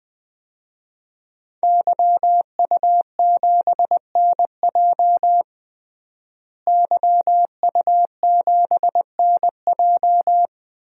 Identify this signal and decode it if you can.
Morse code, heard as a keyed tone, about 20 wpm, 700 Hz. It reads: YU7NJ YU7NJ
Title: DXCC-100 Call Signs - 5 Characters - Repeated 2x - 20wpm